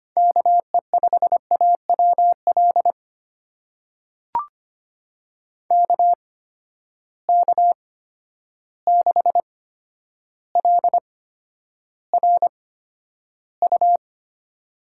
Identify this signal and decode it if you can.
Morse code, heard as a keyed tone, about 25 wpm, spaced as for 11 wpm, 700 Hz. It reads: KE5AWL KK6LRU
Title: USA Call Signs - 8x Character Spacing - 25wpm